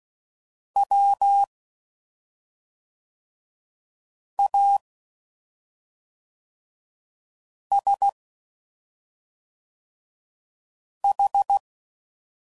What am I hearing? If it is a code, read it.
Morse code, heard as a keyed tone, about 16 wpm, 800 Hz. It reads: W A S H
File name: Suffixes / Special ING's / SlowToFaster ING's